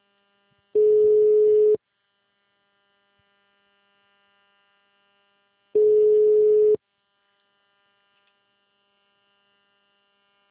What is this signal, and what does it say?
Ringback tone (Europe, 425 Hz).
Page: Cornelio Procopio 2